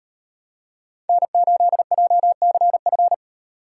Morse code, heard as a keyed tone, about 31 wpm, 700 Hz. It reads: N8JCF
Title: Speed Racing - USA 1x3 Call Signs 25wpm